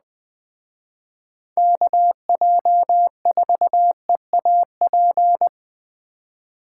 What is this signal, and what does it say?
Morse code, heard as a keyed tone, about 20 wpm, 700 Hz. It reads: KJ4EAP
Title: Speed Racing - USA 2x3 Call Signs 17wpm